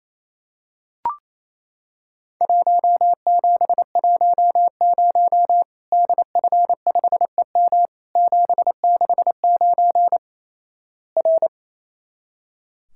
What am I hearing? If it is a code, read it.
Morse code, heard as a keyed tone, about 28 wpm, 700 Hz. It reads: 1710 DF5EM 769 R
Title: Worked All Europe DX Contest - QTC Exchange - 28wpm